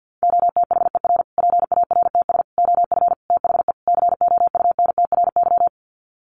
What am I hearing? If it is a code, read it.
Morse code, heard as a keyed tone, about 50 wpm, 700 Hz. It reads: OTHER PARTS OF THE COUNTRY
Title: Sets of 5 Words Encore 50wpm